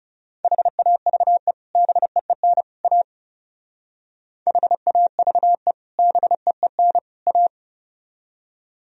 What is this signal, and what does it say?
Morse code, heard as a keyed tone, about 34 wpm, 700 Hz. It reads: HAVE BEEN A HAVE BEEN A
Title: Speed-Racing - Sets of 3 Words 30wpm